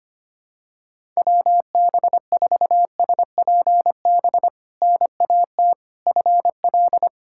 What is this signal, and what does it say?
Morse code, heard as a keyed tone, about 25 wpm, 700 Hz. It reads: WB4SPB NAT FL
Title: K1USN SST QSO Practice - 25wpm